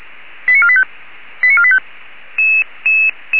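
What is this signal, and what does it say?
Five-tone selective call (ZVEI), heard as quick runs of five tones, tones of about 70 ms. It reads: 89485 89485
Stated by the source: Five-tone ZVEI selective call 89496, received on SDR, from the feuerwehr-orlamuende.org audio archive.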